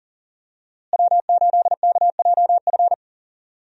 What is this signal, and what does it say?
Morse code, heard as a keyed tone, about 40 wpm, 700 Hz. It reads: W8KJF
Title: USA Call Signs - 5 Characters - Repeated 3x - 40wpm